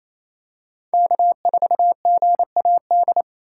Morse code, heard as a keyed tone, about 28 wpm, 700 Hz. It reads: K4GAD
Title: USA Call Signs - 5 Characters - 2x Character Spacing - 28wpm